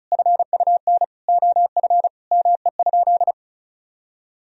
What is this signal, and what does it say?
Morse code, heard as a keyed tone, about 36 wpm, 700 Hz. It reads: FUN OF ME?
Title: Sentences from Top 1000 Words - Review Part 4 - 35wpm